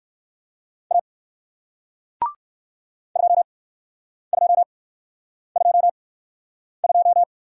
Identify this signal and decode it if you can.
Morse code, heard as a keyed tone, about 61 wpm, 700 Hz. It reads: A 2 2 2 2